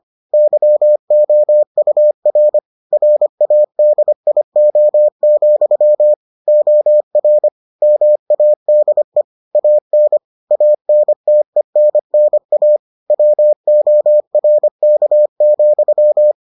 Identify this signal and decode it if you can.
Morse code, heard as a keyed tone, about 25 wpm, 600 Hz. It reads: YOUR RADIO, OR MADE AN ANTENNA WORK,